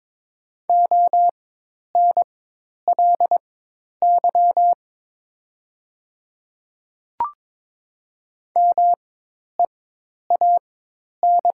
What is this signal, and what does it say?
Morse code, heard as a keyed tone, about 22 wpm, spaced as for 10 wpm, 700 Hz. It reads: ONLY MEAN
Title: 4x Character Spacing - Top 500 Words - 22wpm